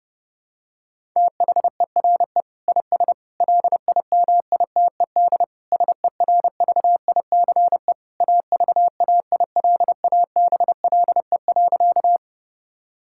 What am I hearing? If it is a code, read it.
Morse code, heard as a keyed tone, about 28 wpm, 700 Hz. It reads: THERE IS LIMITED SERVICE AVAILABLE.
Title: Sentences from Top 1000 Words - Review Part 6 - 30wpm